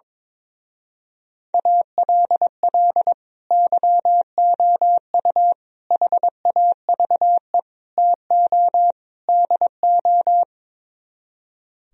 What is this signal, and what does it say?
Morse code, heard as a keyed tone, about 23 wpm, 700 Hz. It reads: ALL YOU HAVE TO DO
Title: Speed-Racing - Sets of 5 Words 22wpm